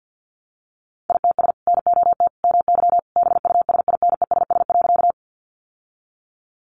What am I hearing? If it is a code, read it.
Morse code, heard as a keyed tone, about 48 wpm, 700 Hz. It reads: ITS NOT MY BUSINESS.
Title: Sentences from Top 600 Words 50wpm